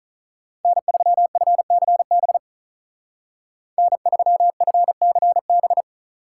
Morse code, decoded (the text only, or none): N3FCB N3FCB